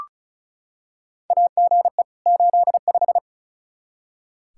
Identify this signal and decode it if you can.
Morse code, heard as a keyed tone, about 36 wpm, 700 Hz. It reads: AGE 85
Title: Practice QSO - Exchange 3 - More Info - 35wpm